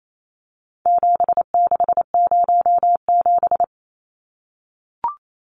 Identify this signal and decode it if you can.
Morse code, heard as a keyed tone, about 28 wpm, 700 Hz. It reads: 7607
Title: Numbers - 4-Digit 28wpm